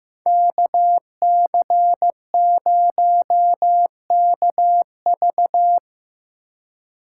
Morse code, heard as a keyed tone, about 17 wpm, 700 Hz. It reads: KC0KV
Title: Speed Racing - USA 2x2 Call Signs 15wpm